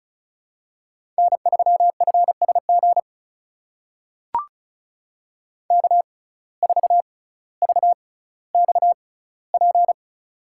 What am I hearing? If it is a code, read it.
Morse code, heard as a keyed tone, about 36 wpm, spaced as for 18 wpm, 700 Hz. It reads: N3FSG K4VXP
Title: USA Call Signs - 5 Characters - 6x Character Spacing - 35wpm